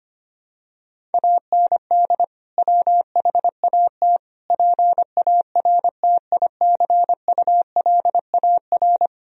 Morse code, heard as a keyed tone, about 25 wpm, 700 Hz. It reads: AND WHAT PARTICULAR
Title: Sentences from Top 1000 Words - Review Part 4 - 25wpm